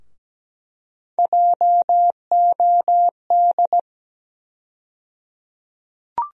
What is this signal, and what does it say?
Morse code, heard as a keyed tone, about 17 wpm, 700 Hz. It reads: JOD
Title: USA Call Sign Postfixes - 3 Letters 17wpm